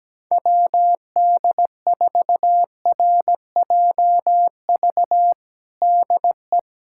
Morse code, heard as a keyed tone, about 17 wpm, 700 Hz. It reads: WD4RJV DE
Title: USA Callsign DE Callsign - 6 Characters - 17wpm